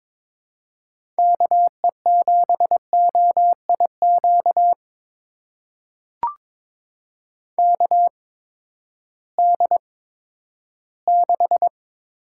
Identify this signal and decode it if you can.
Morse code, heard as a keyed tone, about 22 wpm, spaced as for 13 wpm, 700 Hz. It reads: KE7OIQ KD6
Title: USA Call Signs - 6 Characters - 8x Character Spacing - 22wpm